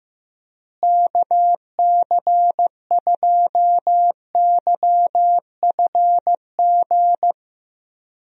Morse code, heard as a keyed tone, about 15 wpm, 700 Hz. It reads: KC2YFG
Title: USA Call Signs - 6 Characters - Repeated 2x - 15wpm